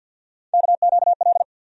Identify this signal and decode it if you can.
Morse code, heard as a keyed tone, about 48 wpm, 700 Hz. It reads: KQL
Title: USA Call Sign Postfixes - 3 Letters 50wpm